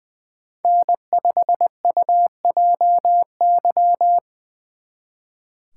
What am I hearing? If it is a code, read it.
Morse code, heard as a keyed tone, about 24 wpm, 700 Hz. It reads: N5UJY
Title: Speed Racing - USA 1x3 Call Signs 20wpm